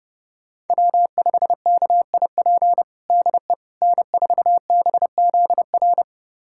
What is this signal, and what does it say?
Morse code, heard as a keyed tone, about 32 wpm, 700 Hz. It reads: W5KIP DE N4BZR
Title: USA Callsign DE Callsign - 5 Characters - 30wpm